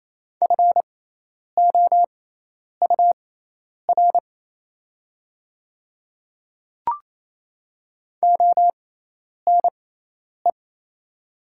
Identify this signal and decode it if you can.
Morse code, heard as a keyed tone, about 27 wpm, spaced as for 10 wpm, 690 Hz. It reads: FOUR ONE
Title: 6x Character Spacing - Top 300 Words - 28wpm